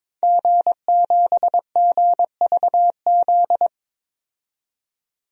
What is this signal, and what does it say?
Morse code, heard as a keyed tone, about 21 wpm, 700 Hz. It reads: G7GVZ
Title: DXCC-100 Call Signs - 5 Characters - 22wpm